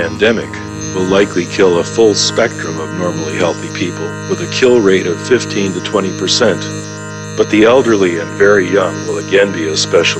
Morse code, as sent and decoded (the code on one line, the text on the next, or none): none